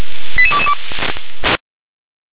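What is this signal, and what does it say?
Five-tone selective call (ZVEI), heard as quick runs of five tones, tones of about 70 ms. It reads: 88222
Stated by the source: Received by SDR.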